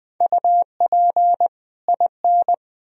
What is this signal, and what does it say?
Morse code, heard as a keyed tone, about 21 wpm, 700 Hz. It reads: UP IN
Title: Speed-Racing - Sets of 2 Words 17wpm